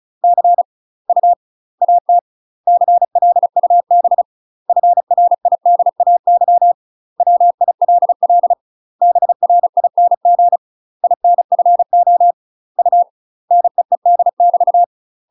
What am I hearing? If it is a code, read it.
Morse code, heard as a keyed tone, about 34 wpm, 700 Hz. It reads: C U AT CLUB FRIDAY WILL BRING INFO U NEED <BT>
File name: UK MASTER QSOs TRAINING / 35WPM / UK MASTER QSOs 0037